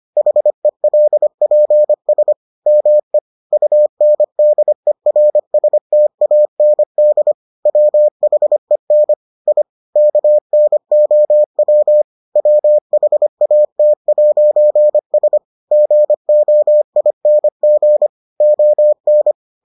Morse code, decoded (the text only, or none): HELPS ME UNDERSTAND WHEN I KNOW WHAT'S GOING ON